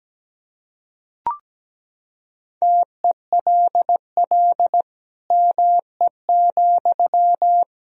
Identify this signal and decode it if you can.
Morse code, heard as a keyed tone, about 17 wpm, 700 Hz. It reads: TELL ME,
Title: Sentences from Top 1000 Words - Review Part 2 - 17wpm